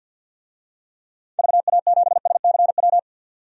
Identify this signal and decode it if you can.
Morse code, heard as a keyed tone, about 47 wpm, 700 Hz. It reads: VA7SCW
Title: DXCC-100 Call Signs - 6 Characters - 50wpm